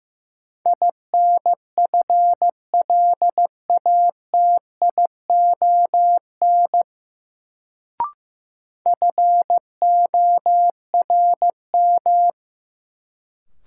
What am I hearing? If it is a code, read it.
Morse code, heard as a keyed tone, about 15 wpm, 700 Hz. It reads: INFLATION FORM
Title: Sentences from Top 1100 Words - New Word Drill Alt - 15wpm